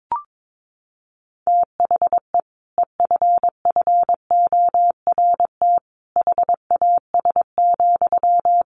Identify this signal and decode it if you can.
Morse code, heard as a keyed tone, about 22 wpm, 700 Hz. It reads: THE EFFORT HAS,
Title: Sentences from Top 1200 Words 22wpm